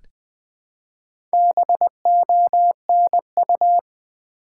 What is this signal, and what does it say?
Morse code, heard as a keyed tone, about 20 wpm, 700 Hz. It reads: BONU
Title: Build A Word - 1 Letter At A Time - 3 to 5-Letter Words - 20wpm